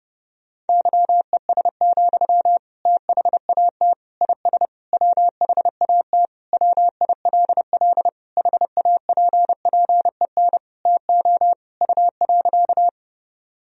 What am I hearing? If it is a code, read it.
Morse code, heard as a keyed tone, about 32 wpm, 700 Hz. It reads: YES, THAT IS WHAT WILL HAPPEN TO U.
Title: Sentences from Top 500 Words and Abbreviations 30wpm